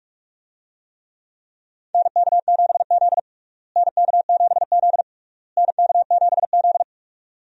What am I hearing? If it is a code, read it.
Morse code, heard as a keyed tone, about 47 wpm, 700 Hz. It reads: NK7Z NK7Z NK7Z
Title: USA Call Signs - 4 Characters - Repeated 3x - 45wpm